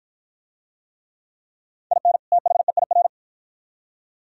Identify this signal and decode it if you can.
Morse code, heard as a keyed tone, about 52 wpm, 700 Hz. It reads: IN THEIR